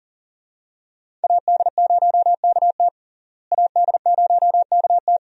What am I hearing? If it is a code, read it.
Morse code, heard as a keyed tone, about 37 wpm, 700 Hz. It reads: AD0KT AD0KT
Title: USA Call Signs - 5 Characters - Repeated 2x - 40wpm